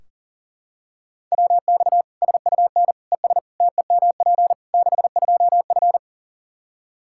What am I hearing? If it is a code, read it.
Morse code, heard as a keyed tone, about 40 wpm, 700 Hz. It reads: WX SUN ES TEMP 62F
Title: Practice QSO - Exchange 2 - Info - 40wpm